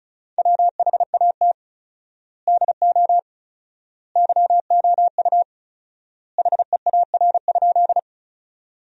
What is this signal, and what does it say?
Morse code, heard as a keyed tone, about 36 wpm, 700 Hz. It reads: WHAT DO YOU HEAR?
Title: 4x Word-Spacing - Sentences from Top 300 Words - 35wpm